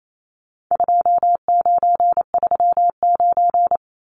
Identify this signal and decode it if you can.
Morse code, heard as a keyed tone, about 28 wpm, 700 Hz. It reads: 2939
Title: Numbers - 4-Digit 28wpm